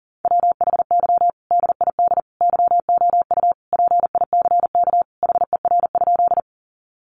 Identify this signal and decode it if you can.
Morse code, heard as a keyed tone, about 37 wpm, 700 Hz. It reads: WHY DID YOU PICK HER?
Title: Sentences from Top 1000 Words 40wpm